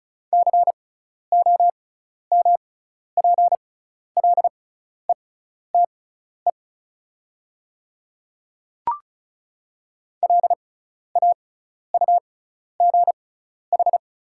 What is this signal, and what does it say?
Morse code, heard as a keyed tone, about 36 wpm, spaced as for 12 wpm, 700 Hz. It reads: COMPLETE LAUGH
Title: 6x Character Spacing - Top 500 Words - 35wpm